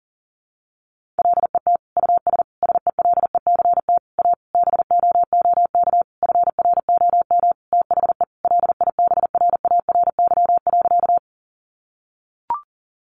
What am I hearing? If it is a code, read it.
Morse code, heard as a keyed tone, about 40 wpm, 700 Hz. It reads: LET US SELECT A BOOK FROM THE LIBRARY.
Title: Sentences from Top 1300 Words 40wpm